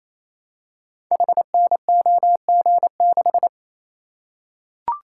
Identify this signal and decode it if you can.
Morse code, heard as a keyed tone, about 29 wpm, 700 Hz. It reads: HNOG6
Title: Code Groups - 5 Letter:Numbers - 28wpm